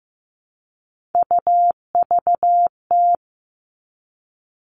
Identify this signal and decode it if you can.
Morse code, heard as a keyed tone, about 15 wpm, 700 Hz. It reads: UVT
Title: All Cut Numbers - 3-Digit 15wpm